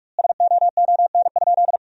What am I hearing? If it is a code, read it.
Morse code, heard as a keyed tone, about 43 wpm, 700 Hz. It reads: SOON?
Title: 2x Word-Spacing - Sentences from Top 400 Words - 45wpm